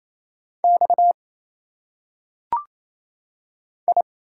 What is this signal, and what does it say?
Morse code, heard as a keyed tone, about 27 wpm, 700 Hz. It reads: X I
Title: Single Letters - Phonetic Pronunciation 28wpm